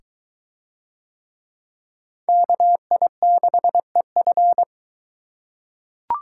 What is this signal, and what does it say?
Morse code, heard as a keyed tone, about 21 wpm, 700 Hz. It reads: KI6EF